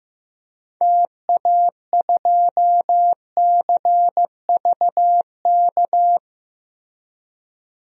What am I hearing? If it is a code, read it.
Morse code, heard as a keyed tone, about 15 wpm, 700 Hz. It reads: TA2CVK
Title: DXCC-100 Call Signs - 6 Characters - Repeated 3x - 15wpm